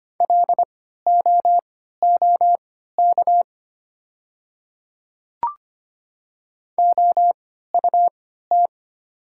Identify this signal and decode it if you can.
Morse code, heard as a keyed tone, about 25 wpm, spaced as for 14 wpm, 700 Hz. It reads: LOOK OUT
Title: 3x Character Spacing - Top 100 Words - 25wpm